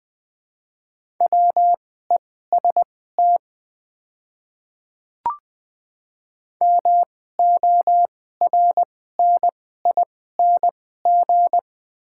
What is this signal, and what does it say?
Morse code, heard as a keyed tone, about 20 wpm, spaced as for 14 wpm, 700 Hz. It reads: WEST MORNING
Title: 2x Character Spacing - Top 500 Words - 20wpm